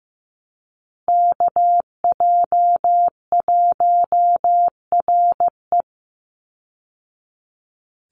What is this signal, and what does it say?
Morse code, heard as a keyed tone, about 15 wpm, 700 Hz. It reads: KJ1RE
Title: Call Signs Repeated 3x - 15wpm